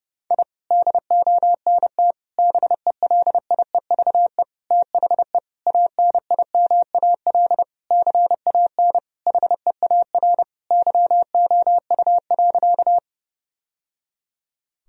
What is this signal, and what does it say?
Morse code, heard as a keyed tone, about 30 wpm, 700 Hz. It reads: I DONT BELIEVE THE ANIMAL CAN HEAR YOU.
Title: Sentences from Top 1000 Words - Review Part 1 - 30wpm